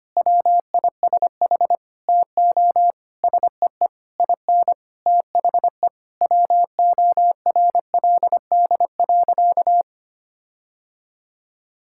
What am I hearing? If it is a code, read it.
Morse code, heard as a keyed tone, about 25 wpm, 700 Hz. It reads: WISH TO SEE IN THE WORLD.
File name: practice / 25wpm / Sentences with 9-Word Phrases - 25wpm